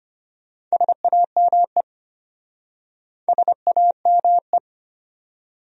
Morse code, heard as a keyed tone, about 33 wpm, 700 Hz. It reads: SAME SAME